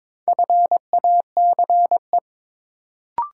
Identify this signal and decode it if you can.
Morse code, heard as a keyed tone, about 22 wpm, 700 Hz. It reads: FACE
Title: Sentences from Top 300 Words - New Word Drill - 22wpm